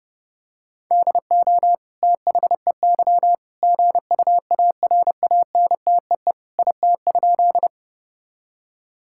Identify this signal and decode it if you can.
Morse code, heard as a keyed tone, about 30 wpm, 700 Hz. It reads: DO THEY GUARANTEE IT?
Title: Sentences from Top 1600 Words 30wpm